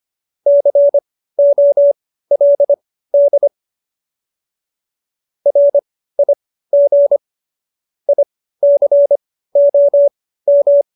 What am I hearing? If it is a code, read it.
Morse code, heard as a keyed tone, about 25 wpm, spaced as for 15 wpm, 570 Hz. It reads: COLD RIG ICOM
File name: qso222 15